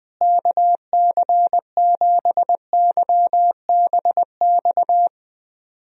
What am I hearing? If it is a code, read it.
Morse code, heard as a keyed tone, about 20 wpm, 700 Hz. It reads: KC7YBX
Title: USA Call Signs - 6 Characters - Repeated 3x - 20wpm